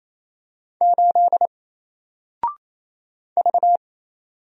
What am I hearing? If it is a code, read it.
Morse code, heard as a keyed tone, about 27 wpm, 700 Hz. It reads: 8 V
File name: Single-Letter-Number-28wpm